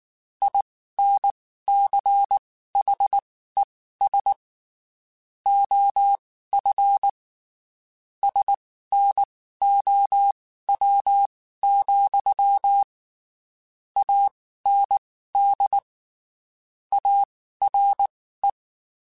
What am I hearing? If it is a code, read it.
Morse code, heard as a keyed tone, about 19 wpm, spaced as for 14 wpm, 800 Hz. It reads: INCHES OF SNOW, AND ARE
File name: (spaced) / QSOs Spaced-mp3s / WA7QOX 1 @18 sp